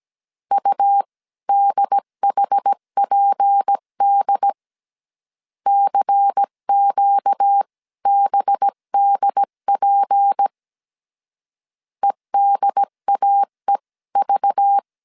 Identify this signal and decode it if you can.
Morse code, heard as a keyed tone, about 17 wpm, spaced as for 15 wpm, 800 Hz. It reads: UDHPD CQBDP EDAEV
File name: CW / DET / 1 6 1500C S C 16wpm 2733